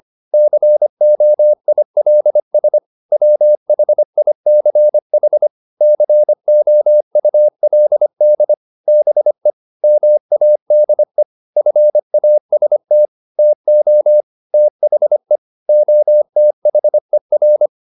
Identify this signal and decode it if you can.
Morse code, heard as a keyed tone, about 25 wpm, 600 Hz. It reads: COILS WHICH COULD BE MADE FAST TO THE OTHER
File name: the old man and the sea part 1 25wpm